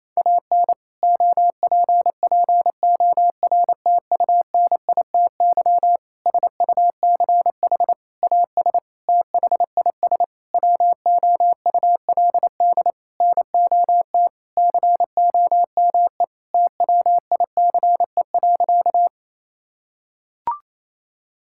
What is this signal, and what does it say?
Morse code, heard as a keyed tone, about 28 wpm, 700 Hz. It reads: AN OPPORTUNITY SUCH AS THIS WOULD NOT COME TWICE.
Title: Sentences from Top 1600 Words 28wpm